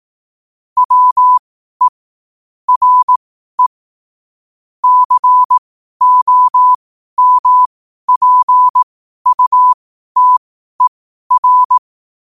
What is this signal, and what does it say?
Morse code, heard as a keyed tone, about 18 wpm, spaced as for 12 wpm, 1000 Hz. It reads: WE RE COMPUTER
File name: qotdfiles / 12wpm